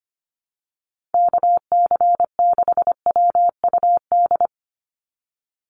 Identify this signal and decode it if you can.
Morse code, heard as a keyed tone, about 25 wpm, 700 Hz. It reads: KC6WUD
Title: USA Call Signs 25wpm